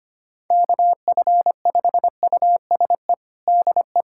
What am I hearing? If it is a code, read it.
Morse code, heard as a keyed tone, about 25 wpm, 700 Hz. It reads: KF5USE DE
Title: USA Callsign DE Callsign 25wpm